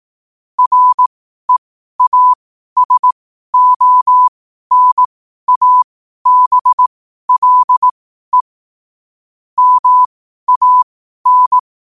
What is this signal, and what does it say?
Morse code, heard as a keyed tone, about 18 wpm, spaced as for 12 wpm, 1000 Hz. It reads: REASONABLE MAN
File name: qotdfiles / 12wpm